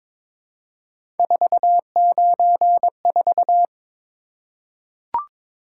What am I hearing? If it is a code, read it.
Morse code, heard as a keyed tone, about 23 wpm, 700 Hz. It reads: 494
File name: practice / 22wpm / Three Letter-Number Combinations - 22wpm